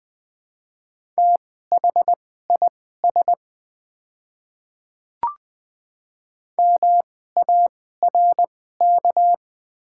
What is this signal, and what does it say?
Morse code, heard as a keyed tone, about 20 wpm, spaced as for 14 wpm, 690 Hz. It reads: THIS MARK
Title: 2x Character Spacing - Top 500 Words - 20wpm